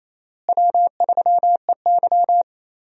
Morse code, heard as a keyed tone, about 28 wpm, 700 Hz. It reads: W3EY
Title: USA Call Signs - 4 Characters - Repeated 2x - 28wpm